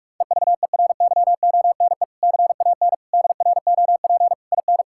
Morse code, heard as a keyed tone, about 41 wpm, 700 Hz. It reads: EVERYONE CAN DROP IN
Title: Sentences from Top 1600 Words 45wpm